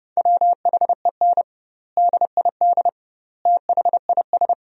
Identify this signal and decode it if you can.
Morse code, heard as a keyed tone, about 30 wpm, 700 Hz. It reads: WHEN DID THIS
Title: 2x Word-Spacing - Sentences from Top 400 Words - 30wpm